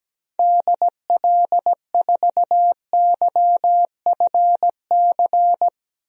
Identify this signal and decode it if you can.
Morse code, heard as a keyed tone, about 17 wpm, 700 Hz. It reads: DL4YFC
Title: DXCC-100 Call Signs - 6 Characters - 17wpm